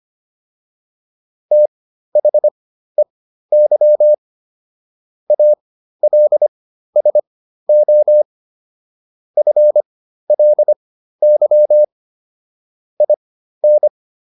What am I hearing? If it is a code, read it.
Morse code, heard as a keyed tone, about 25 wpm, spaced as for 13 wpm, 600 Hz. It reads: THEY ALSO FLY IN